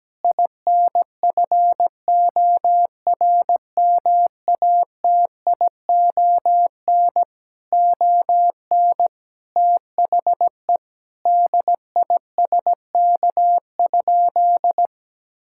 Morse code, heard as a keyed tone, about 17 wpm, 700 Hz. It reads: INFORMATION ON THE DISK?
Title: Sentences from Top 1000 Words - Review Part 6 - 17wpm